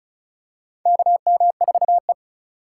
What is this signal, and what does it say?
Morse code, heard as a keyed tone, about 34 wpm, 700 Hz. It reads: KM4E
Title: USA Call Signs - 4 Characters - 2x Character Spacing - 35wpm